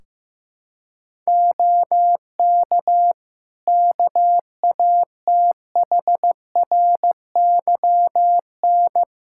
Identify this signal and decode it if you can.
Morse code, heard as a keyed tone, about 16 wpm, 700 Hz. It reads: OK KATHRYN
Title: K1USN SST QSO Practice 15wpm